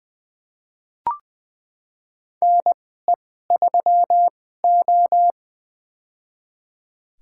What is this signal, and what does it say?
Morse code, heard as a keyed tone, about 20 wpm, spaced as for 14 wpm, 700 Hz. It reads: NE3O
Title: USA Call Signs - 4 Characters - 2x Character Spacing - 20wpm